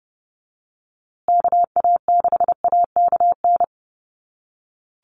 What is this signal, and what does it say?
Morse code, heard as a keyed tone, about 30 wpm, 700 Hz. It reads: KA6AKN
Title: USA Call Signs Repeated 3x 30wpm